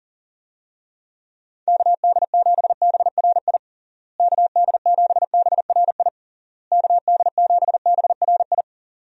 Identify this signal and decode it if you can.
Morse code, heard as a keyed tone, about 40 wpm, 700 Hz. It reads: KD7BRI KD7BRI KD7BRI
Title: USA Call Signs - 6 Characters - Repeated 3x - 40wpm